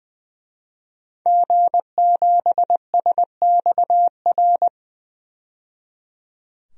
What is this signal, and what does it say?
Morse code, heard as a keyed tone, about 21 wpm, 700 Hz. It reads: G7SXR
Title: DXCC-100 Call Signs - 5 Characters - 20wpm